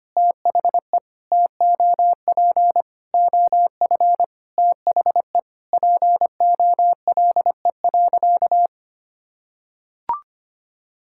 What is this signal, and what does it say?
Morse code, heard as a keyed tone, about 25 wpm, 700 Hz. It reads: THE TOP OF THE POLE.
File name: practice / 25wpm / Sentences with 4-Word Phrases - 25wpm